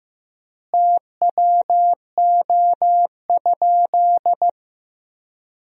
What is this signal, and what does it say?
Morse code, heard as a keyed tone, about 15 wpm, 700 Hz. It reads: TWO?
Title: 3x Word-Spacing - Sentences from Top 100 Words - 15wpm